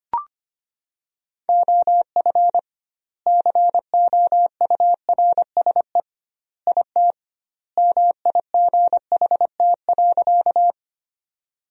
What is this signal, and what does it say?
Morse code, heard as a keyed tone, about 25 wpm, 700 Hz. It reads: OF COURSE IT MIGHT.
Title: 2x Word-Spacing - Sentences from Top 500 Words - 25wpm